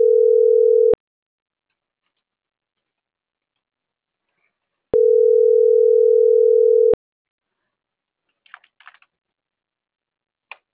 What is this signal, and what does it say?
Ringback tone (North America), 440 + 480 Hz.